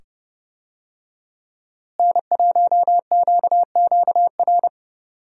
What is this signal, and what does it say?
Morse code, heard as a keyed tone, about 30 wpm, 700 Hz. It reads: N1QQR
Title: USA Call Signs - 5 Characters - Repeated 2x - 30wpm